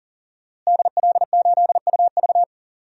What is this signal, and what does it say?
Morse code, heard as a keyed tone, about 40 wpm, 700 Hz. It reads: DL8UV
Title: DXCC-100 Call Signs - 5 Characters - Repeated 2x - 40wpm